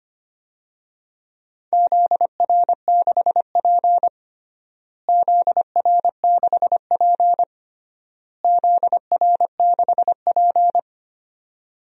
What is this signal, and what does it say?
Morse code, heard as a keyed tone, about 25 wpm, 700 Hz. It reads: ZR6P ZR6P ZR6P
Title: DXCC-100 Call Signs - 4 Characters - Repeated 3x - 25wpm